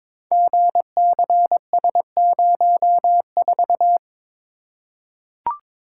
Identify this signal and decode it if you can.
Morse code, heard as a keyed tone, about 22 wpm, 700 Hz. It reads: GCS04
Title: Code Groups - 5 Letter:Numbers - 22wpm